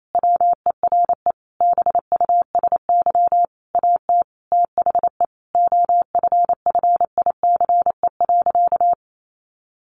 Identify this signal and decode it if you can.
Morse code, heard as a keyed tone, about 28 wpm, 700 Hz. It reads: WERE BUSY AT THE OFFICE.
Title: Sentences from Top 1500 Words 28wpm